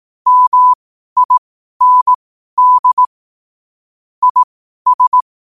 Morse code, heard as a keyed tone, about 18 wpm, spaced as for 12 wpm, 1000 Hz. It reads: MIND IS
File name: qotdfiles / 12wpm